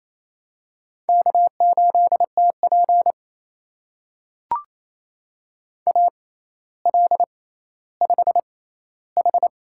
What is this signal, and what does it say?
Morse code, heard as a keyed tone, about 28 wpm, spaced as for 14 wpm, 700 Hz. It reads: K8TP AL5H
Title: USA Call Signs - 6x Character Spacing - 28wpm